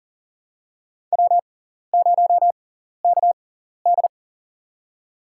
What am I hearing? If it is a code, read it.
Morse code, heard as a keyed tone, about 37 wpm, spaced as for 13 wpm, 700 Hz. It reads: W0KD
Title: USA Call Signs - 6x Character Spacing - 40wpm